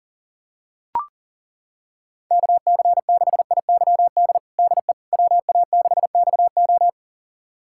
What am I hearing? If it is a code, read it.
Morse code, heard as a keyed tone, about 40 wpm, 700 Hz. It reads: KC6IYD DE WA6XO
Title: USA Callsign DE Callsign 40wpm